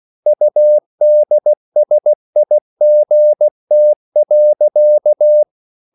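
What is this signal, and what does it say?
Morse code, heard as a keyed tone, about 16 wpm, 600 Hz. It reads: UDSIGT.
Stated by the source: Slow (QRS) Morse code at 16 wpm, from CW med Gnister.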